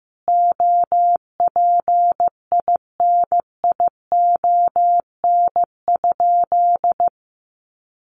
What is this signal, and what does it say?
Morse code, heard as a keyed tone, about 15 wpm, 700 Hz. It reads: OPINION?
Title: Sentences from Top 1200 Words 15wpm